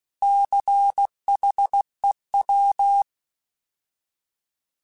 Morse code, heard as a keyed tone, about 16 wpm, 800 Hz. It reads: CHEW
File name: Alphabetical / Alphabetical mp3s / Cs @15